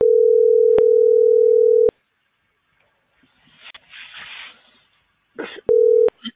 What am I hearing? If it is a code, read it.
Ringback tone (North America), 440 + 480 Hz.